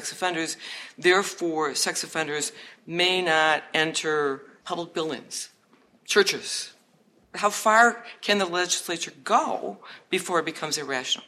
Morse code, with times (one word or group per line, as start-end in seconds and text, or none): none